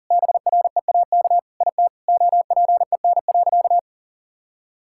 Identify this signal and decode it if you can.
Morse code, heard as a keyed tone, about 37 wpm, 700 Hz. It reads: BREAK IT OPEN.
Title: Sentences from Top 1000 Words - Review Part 6 - 40wpm